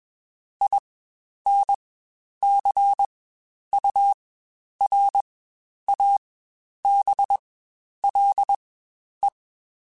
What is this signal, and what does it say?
Morse code, heard as a keyed tone, about 21 wpm, spaced as for 10 wpm, 800 Hz. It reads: INCURABLE
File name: Quotes Spaced / Quotes Spaced-mp3s / Optimist sp @20